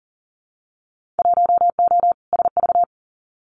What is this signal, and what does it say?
Morse code, heard as a keyed tone, about 38 wpm, 700 Hz. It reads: 1O SV
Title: Winter Field Day Exchanges 40wpm